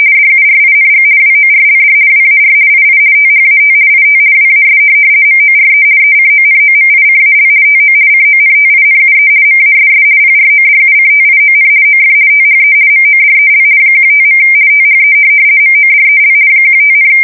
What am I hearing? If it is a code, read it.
RTTY, about 50 baud. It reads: DURING WORLD WAR II THAT IT WAS MORE ECONOMICAL TO CONTINUE MASS PRODUCTION OF THE MODEL 15. THE MODEL 15,
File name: fsk170-50B / teletype-corp